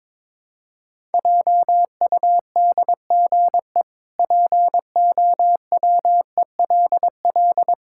Morse code, heard as a keyed tone, about 22 wpm, 700 Hz. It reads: JUDGE POWELL
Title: Pangram Challenge v2 - 22wpm